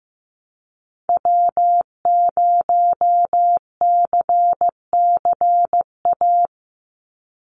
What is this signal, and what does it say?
Morse code, heard as a keyed tone, about 15 wpm, 700 Hz. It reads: W0CCA
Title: NA SOTA Activator Callsigns 15wpm